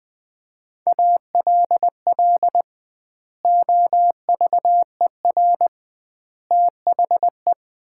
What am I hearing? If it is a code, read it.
Morse code, heard as a keyed tone, about 20 wpm, 700 Hz. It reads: ALL OVER THE
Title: 2x Word-Spacing - Sets of 4 Words - 20wpm